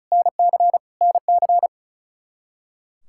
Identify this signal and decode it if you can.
Morse code, heard as a keyed tone, about 36 wpm, 700 Hz. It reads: NC NC
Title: Practice QSO - Exchange 1 - Beginning - 35wpm